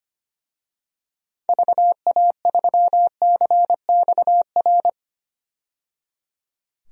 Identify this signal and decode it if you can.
Morse code, heard as a keyed tone, about 25 wpm, 700 Hz. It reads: VA3CXR
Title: DXCC-100 Call Signs - 6 Characters - 25wpm